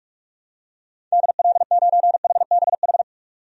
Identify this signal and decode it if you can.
Morse code, heard as a keyed tone, about 42 wpm, 700 Hz. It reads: DL9HBH
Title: DXCC-100 Call Signs - 6 Characters - 45wpm